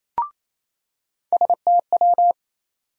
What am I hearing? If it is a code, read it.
Morse code, heard as a keyed tone, about 28 wpm, 700 Hz. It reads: STW
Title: USA Call Sign Postfixes - 3 Letters 28wpm